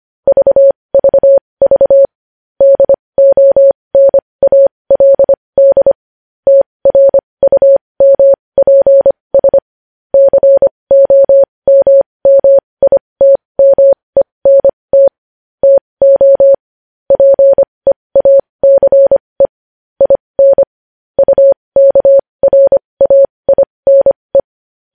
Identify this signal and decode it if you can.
Morse code, heard as a keyed tone, about 25 wpm, spaced as for 20 wpm, 550 Hz. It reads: VVV DONALD TRUMPS COMMITMENT TO PEACE IN UKRAINE
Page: news-headlines-in-morse-code-at-20-wpm